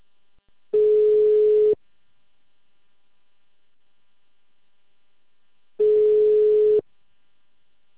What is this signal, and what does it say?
Ringback tone (Europe, 425 Hz).
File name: out- -2044- - - .29486